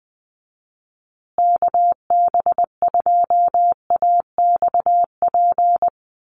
Morse code, heard as a keyed tone, about 20 wpm, 700 Hz. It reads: KB2AXP
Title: USA Call Signs Repeated 2x 20wpm